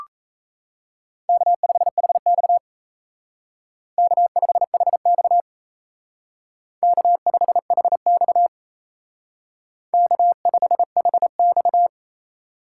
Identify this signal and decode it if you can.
Morse code, heard as a keyed tone, about 36 wpm, 700 Hz. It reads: K5HX K5HX K5HX K5HX